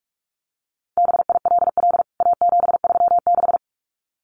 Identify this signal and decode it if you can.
Morse code, heard as a keyed tone, about 44 wpm, 700 Hz. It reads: BILL A736
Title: CWT Exchange 45wpm